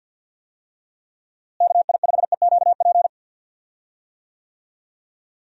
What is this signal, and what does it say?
Morse code, heard as a keyed tone, about 46 wpm, 700 Hz. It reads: KI5EQP
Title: USA Call Signs - 6 Characters - Repeated 3x - 50wpm